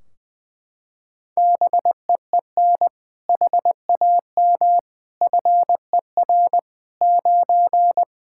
Morse code, heard as a keyed tone, about 20 wpm, 700 Hz. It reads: BEEN HAM FER 9
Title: Practice QSO - Exchange 3 - More Info - 20wpm